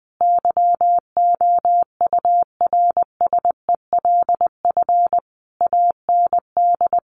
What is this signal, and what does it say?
Morse code, heard as a keyed tone, about 20 wpm, 700 Hz. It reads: YOURSELF AND
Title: Sentences from Top 900 Words 20wpm